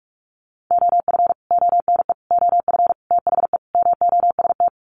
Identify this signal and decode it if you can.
Morse code, heard as a keyed tone, about 45 wpm, 700 Hz. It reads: OF ONE OF THE MOST
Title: Sets of 5 Words Encore2 45wpm